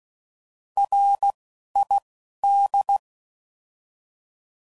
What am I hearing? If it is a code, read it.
Morse code, heard as a keyed tone, about 16 wpm, 800 Hz. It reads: R I D
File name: Prefixes / By First Letter / Rs